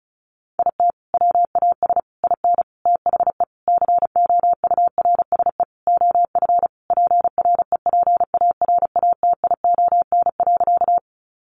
Morse code, heard as a keyed tone, about 34 wpm, 700 Hz. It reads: IT WAS IN THE COURSE OF PREPARATION.
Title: Sentences from Top 1400 Words 35wpm